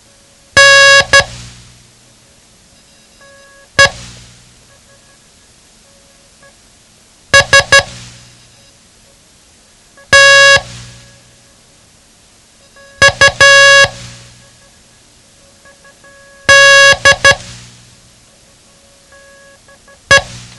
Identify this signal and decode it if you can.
Morse code, heard as a keyed tone, about 12 wpm, 1650 Hz. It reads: N E S T U D E
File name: ham radio / CodeQuick / Code Quick 07